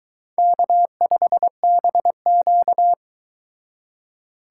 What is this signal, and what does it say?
Morse code, heard as a keyed tone, about 24 wpm, 700 Hz. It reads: K5BQ